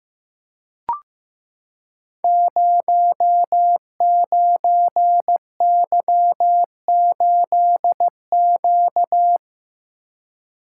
Morse code, heard as a keyed tone, about 15 wpm, 700 Hz. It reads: 09Y8Q
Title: Code Groups - 5 Letter:Numbers - 15wpm